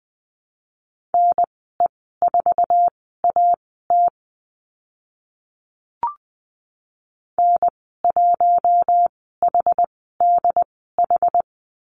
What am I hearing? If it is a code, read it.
Morse code, heard as a keyed tone, about 20 wpm, spaced as for 14 wpm, 700 Hz. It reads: NE4AT N1HDH
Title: 076 Lesson 13g - Letter L - Call Signs 20wpm